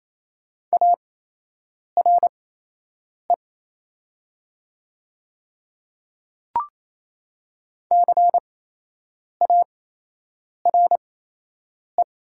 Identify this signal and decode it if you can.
Morse code, heard as a keyed tone, about 28 wpm, spaced as for 8 wpm, 700 Hz. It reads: ARE CARE
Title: 8x Character Spacing - Top 500 Words - 28wpm